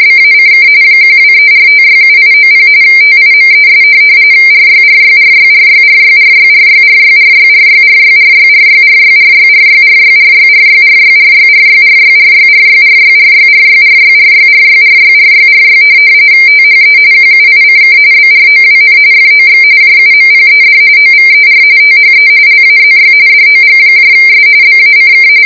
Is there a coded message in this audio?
Yes: RTTY from 0.0 to 25.3 s, about 45.45 baud. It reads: BACK 1234567890 TIMES RYRYRYRYRYRYRYRYRYRYRYRYRYRYRYRYRYRYRYRYRYRYRYRYRYRYRYRYRYRYRYRYRYRY THE QUICK BROWN FOX JUMPED OVER THE LAZY DOGS BACK